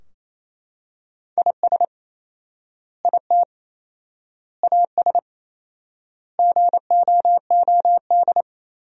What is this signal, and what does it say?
Morse code, heard as a keyed tone, about 28 wpm, 700 Hz. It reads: IS IT AS GOOD
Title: 4x Word-Spacing - Sentences from Top 200 Words - 28wpm